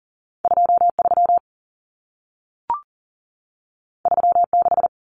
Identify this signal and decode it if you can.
Morse code, heard as a keyed tone, about 40 wpm, 700 Hz. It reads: 23 36
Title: Numbers - 2-Digit 40wpm